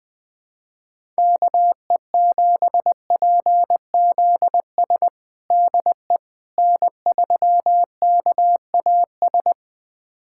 Morse code, heard as a keyed tone, about 20 wpm, 690 Hz. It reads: KE7PZS DE N3KAS
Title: USA Callsign DE Callsign 20wpm